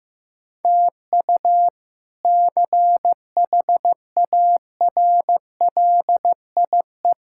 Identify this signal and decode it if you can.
Morse code, heard as a keyed tone, about 15 wpm, 700 Hz. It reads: TU CHARLIE
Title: K1USN SST QSO Practice - 15wpm